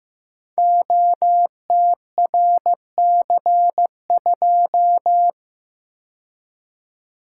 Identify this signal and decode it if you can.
Morse code, heard as a keyed tone, about 15 wpm, 700 Hz. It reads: OTRC2
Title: Code Groups - 5 Letter:Numbers - 15wpm